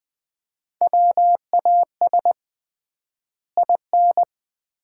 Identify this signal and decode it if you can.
Morse code, heard as a keyed tone, about 20 wpm, 700 Hz. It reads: WAS IN
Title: 3x Word-Spacing - Sets of 2 Words - 20wpm